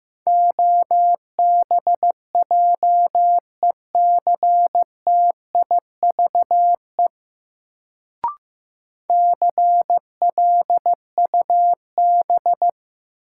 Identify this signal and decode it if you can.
Morse code, heard as a keyed tone, about 15 wpm, 700 Hz. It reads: OBJECTIVE CLUB
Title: Sentences from Top 1200 Words - New Word Drill - 15wpm